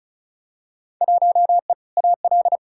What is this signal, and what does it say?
Morse code, heard as a keyed tone, about 34 wpm, 700 Hz. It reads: 1E AL
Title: Field Day Exchanges - 35wpm